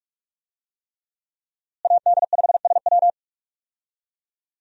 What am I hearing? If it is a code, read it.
Morse code, heard as a keyed tone, about 46 wpm, 700 Hz. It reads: AD5SW
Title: USA Call Signs - 5 Characters - Repeated 3x - 45wpm